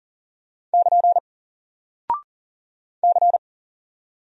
Morse code, heard as a keyed tone, about 39 wpm, 700 Hz. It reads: <KN> C
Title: Single Character (Letter:Number:Punctuation:Prosign) Alt 40wpm